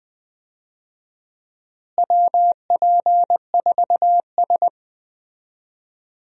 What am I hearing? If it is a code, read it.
Morse code, heard as a keyed tone, about 21 wpm, 700 Hz. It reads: WP4S
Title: USA Call Signs - 4 Characters - Repeated 3x - 20wpm